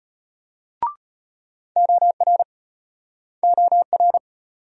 Morse code, heard as a keyed tone, about 34 wpm, 700 Hz. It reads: OR OR